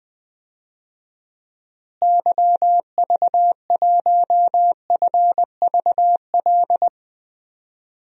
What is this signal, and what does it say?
Morse code, heard as a keyed tone, about 20 wpm, 700 Hz. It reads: YV1FVL
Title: DXCC-100 Call Signs - 6 Characters - Repeated 2x - 20wpm